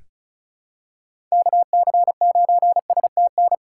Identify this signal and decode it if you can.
Morse code, heard as a keyed tone, about 34 wpm, 700 Hz. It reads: KC9STN
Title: USA Call Signs - 3x Character Spacing - 35wpm